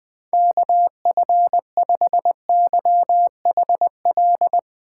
Morse code, heard as a keyed tone, about 20 wpm, 700 Hz. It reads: KF5YHL
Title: USA Call Signs - 6 Characters - 3x Character Spacing - 20wpm